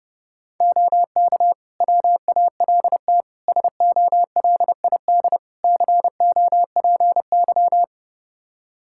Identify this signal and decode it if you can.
Morse code, heard as a keyed tone, about 32 wpm, 700 Hz. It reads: OK WALT SOLID COPY
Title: Practice QSO - Exchange 3 - More Info - 30wpm